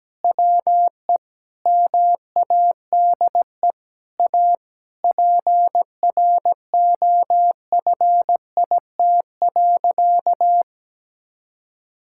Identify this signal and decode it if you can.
Morse code, heard as a keyed tone, about 17 wpm, 700 Hz. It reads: WE MADE A PROFIT.
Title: Sentences from Top 1000 Words - Review Part 5 - 17wpm